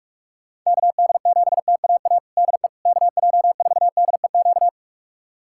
Morse code, heard as a keyed tone, about 44 wpm, 700 Hz. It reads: KD7TAA DE KJ4DEQ
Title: USA Callsign DE Callsign 45wpm